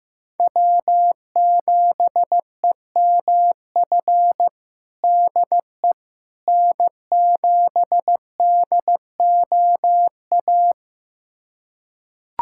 Morse code, heard as a keyed tone, about 15 wpm, 700 Hz. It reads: W7EMF DE N7DOA
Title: USA Callsign DE Callsign - 5 Characters - 15wpm